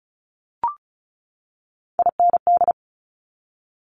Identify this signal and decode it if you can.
Morse code, heard as a keyed tone, about 36 wpm, 700 Hz. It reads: IND
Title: Multi-letter Phonemes 35wpm